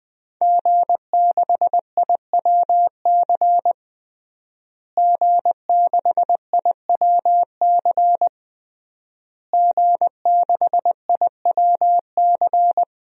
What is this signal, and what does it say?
Morse code, heard as a keyed tone, about 20 wpm, 700 Hz. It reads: G6IWC G6IWC G6IWC
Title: DXCC-100 Call Signs - 5 Characters - Repeated 3x - 20wpm